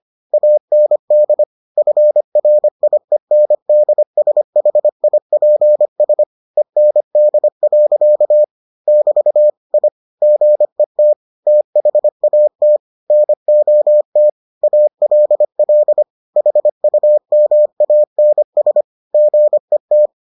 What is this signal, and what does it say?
Morse code, heard as a keyed tone, about 25 wpm, 600 Hz. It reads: AND FRIENDSHIPS END. <BT> I GET THAT NOT ALL HUMANS GET